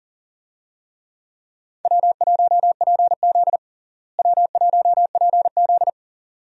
Morse code, heard as a keyed tone, about 40 wpm, 700 Hz. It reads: W1PZ W1PZ
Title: USA Call Signs - 4 Characters - Repeated 2x - 40wpm